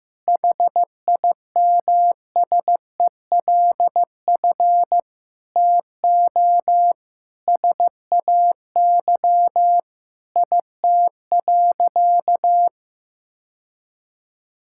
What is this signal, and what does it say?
Morse code, heard as a keyed tone, about 15 wpm, 700 Hz. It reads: HIMSELF TO SAY IT.
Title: Sentences from Top 1600 Words 15wpm